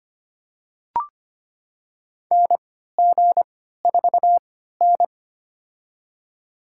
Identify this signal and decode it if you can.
Morse code, heard as a keyed tone, about 25 wpm, spaced as for 14 wpm, 700 Hz. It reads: NG4N
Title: USA Call Signs - 4 Characters - 3x Character Spacing - 25wpm